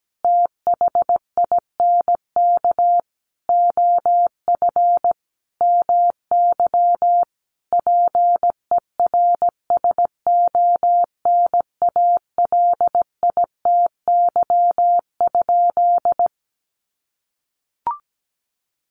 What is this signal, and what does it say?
Morse code, heard as a keyed tone, about 17 wpm, 700 Hz. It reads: THINK OF MY PERSONALITY?
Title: Sentences from Top 1400 Words 17wpm